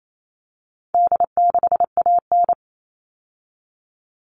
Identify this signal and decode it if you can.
Morse code, heard as a keyed tone, about 28 wpm, 700 Hz. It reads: D6AN
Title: All Cut Numbers - 4-Digit 28wpm